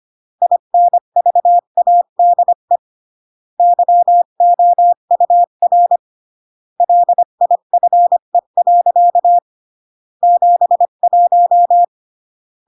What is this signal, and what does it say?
Morse code, heard as a keyed tone, about 25 wpm, 700 Hz. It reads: INVADE YOUR LIFE. 71